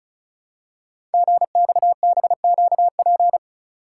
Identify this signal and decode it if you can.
Morse code, heard as a keyed tone, about 36 wpm, 700 Hz. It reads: GXBQP
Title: Code Groups - 5 Letters - 35wpm